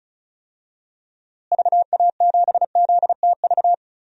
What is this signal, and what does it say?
Morse code, heard as a keyed tone, about 36 wpm, 700 Hz. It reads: VA7ZTV
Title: DXCC-100 Call Signs - 6 Characters - Repeated 2x - 35wpm